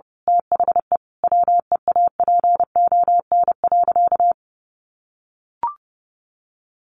Morse code, heard as a keyed tone, about 28 wpm, 700 Hz. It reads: THE WEAPON.
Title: A Sound Of Thunder 30wpm